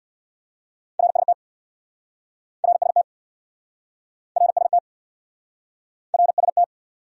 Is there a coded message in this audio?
Yes: Morse code from 1.0 to 6.6 s, heard as a keyed tone, about 69 wpm, 700 Hz. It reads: RST RST RST RST